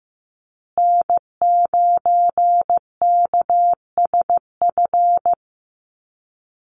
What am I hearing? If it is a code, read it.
Morse code, heard as a keyed tone, about 15 wpm, 700 Hz. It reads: N9KSF
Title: USA Call Signs Repeated 2x 15wpm